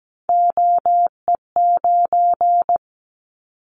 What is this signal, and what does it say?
Morse code, heard as a keyed tone, about 17 wpm, 700 Hz. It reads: OE9
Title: Callsign Prefixes for Top 100 DXCC - 17wpm